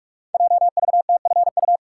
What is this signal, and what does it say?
Morse code, heard as a keyed tone, about 44 wpm, 700 Hz. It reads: JVTFU
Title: Code Groups - 5 Letter:Numbers - 45wpm